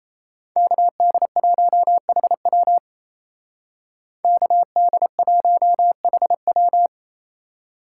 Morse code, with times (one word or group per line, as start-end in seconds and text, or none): 0.6-2.8 KD1HW
4.2-6.9 KD1HW